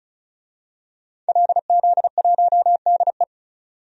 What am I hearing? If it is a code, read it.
Morse code, heard as a keyed tone, about 36 wpm, 700 Hz. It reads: LZ1DE
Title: DXCC-100 Call Signs - 5 Characters - 35wpm